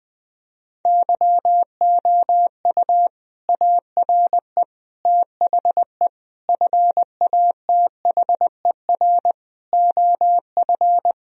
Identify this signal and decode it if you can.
Morse code, heard as a keyed tone, about 20 wpm, 700 Hz. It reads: YOU ARE THE FATHER OF
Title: Sentences from Top 1000 Words - Review Part 2 - 20wpm